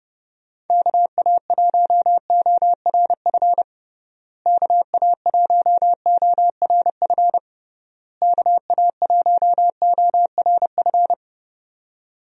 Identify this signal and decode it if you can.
Morse code, heard as a keyed tone, about 30 wpm, 700 Hz. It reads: KA1ORF KA1ORF KA1ORF
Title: USA Call Signs - 6 Characters - Repeated 3x - 30wpm